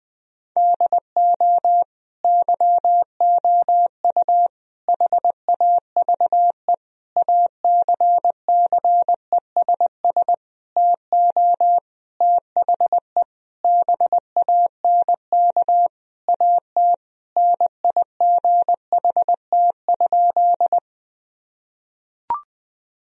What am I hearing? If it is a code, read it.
Morse code, heard as a keyed tone, about 21 wpm, 700 Hz. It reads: DO YOU HAVE ACCESS TO THE BANK AT NIGHT?
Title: Sentences from Top 1000 Words - Review Part 4 - 20wpm